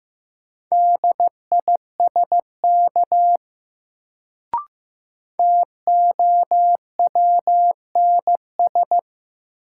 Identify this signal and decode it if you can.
Morse code, heard as a keyed tone, about 15 wpm, 700 Hz. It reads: DISK TOWNS
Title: Sentences from Top 1000 Words - Review Part 6 - New Word Drill - 15wpm